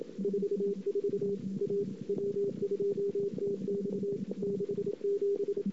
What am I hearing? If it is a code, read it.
Morse code, heard as a keyed tone, about 28 wpm, 400 Hz, with noise. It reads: 44 A W2TX 67
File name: CW / 2008-ss-cw-k5tr-op-n5rz -22:00